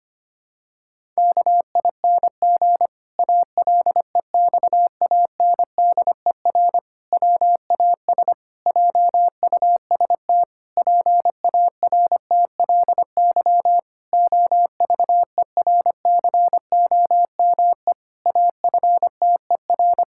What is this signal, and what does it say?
Morse code, heard as a keyed tone, about 25 wpm, 700 Hz. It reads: KING ALEXANDER WAS JUST PARTLY OVERCOME AFTER
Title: Pangram Challenge v2 - 25wpm